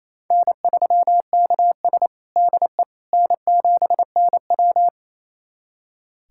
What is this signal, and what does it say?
Morse code, heard as a keyed tone, about 27 wpm, 690 Hz. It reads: N3KS DE N7NW
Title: USA Callsign DE Callsign - 4 Characters - 28wpm